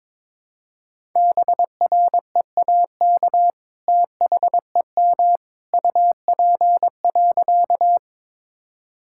Morse code, heard as a keyed tone, about 22 wpm, 700 Hz. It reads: BREAK THEM UP.
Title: Sentences from Top 1000 Words - Review Part 6 - 22wpm